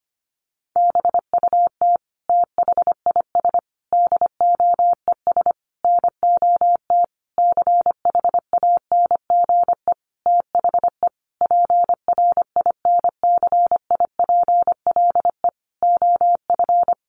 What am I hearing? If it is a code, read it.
Morse code, heard as a keyed tone, about 25 wpm, 700 Hz. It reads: BUT THIS DOES NOT CHANGE THE PRINCIPLE OF
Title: Sentences from Top 1400 Words 25wpm